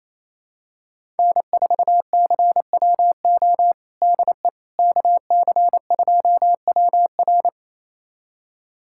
Morse code, heard as a keyed tone, about 28 wpm, 700 Hz. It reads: N4CWO DE KC2WR
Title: USA Callsign DE Callsign - 5 Characters - 28wpm